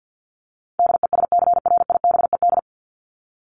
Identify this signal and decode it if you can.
Morse code, heard as a keyed tone, about 50 wpm, 700 Hz. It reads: DESCRIBED
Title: ED Suffix - Top 100 Words with Base Words 50wpm